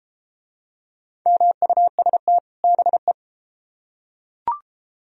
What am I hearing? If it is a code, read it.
Morse code, heard as a keyed tone, about 28 wpm, 700 Hz. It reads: MUST BE